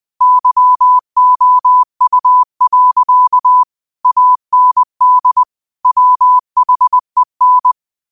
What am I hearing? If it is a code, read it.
Morse code, heard as a keyed tone, about 20 wpm, 1000 Hz. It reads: YOU. AND WHEN
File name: qotdfiles / 20wpm